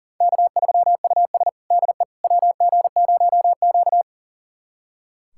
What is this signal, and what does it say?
Morse code, heard as a keyed tone, about 40 wpm, 700 Hz. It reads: K3US DE WG0Q
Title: USA Callsign DE Callsign - 4 Characters - 40wpm